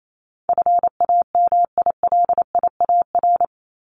Morse code, heard as a keyed tone, about 28 wpm, 700 Hz. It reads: FAMILIAR
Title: Top 751-1000 Words 28wpm